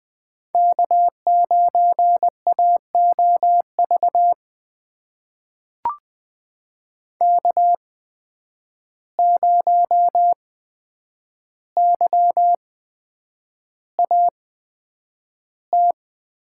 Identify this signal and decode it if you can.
Morse code, heard as a keyed tone, about 20 wpm, spaced as for 9 wpm, 700 Hz. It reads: K9AOV K0YAT
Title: USA Call Signs - 5 Characters - 8x Character Spacing - 20wpm